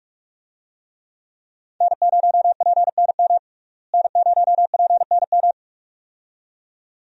Morse code, heard as a keyed tone, about 46 wpm, 700 Hz. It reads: N0PNM N0PNM
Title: USA Call Signs - 5 Characters - Repeated 2x - 45wpm